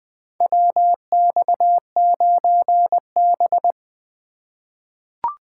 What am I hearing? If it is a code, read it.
Morse code, heard as a keyed tone, about 20 wpm, 700 Hz. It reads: WX9B
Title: USA Call Signs - 4 Characters - 2x Character Spacing - 20wpm